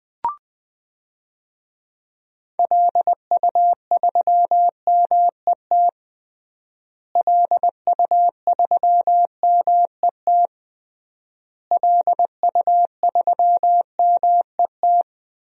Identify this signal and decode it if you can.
Morse code, heard as a keyed tone, about 19 wpm, 700 Hz. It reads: LU3MET LU3MET LU3MET
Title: DXCC-100 Call Signs - 6 Characters - Repeated 3x - 20wpm